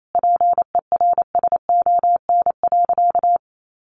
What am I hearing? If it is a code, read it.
Morse code, heard as a keyed tone, about 28 wpm, 700 Hz. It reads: PERSON.
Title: Sentences from Top 1500 Words 28wpm